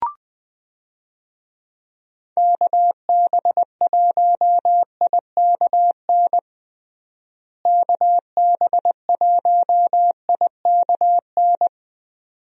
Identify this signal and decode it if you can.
Morse code, heard as a keyed tone, about 19 wpm, 700 Hz. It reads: KB1IKN KB1IKN
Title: USA Call Signs - 6 Characters - Repeated 2x - 20wpm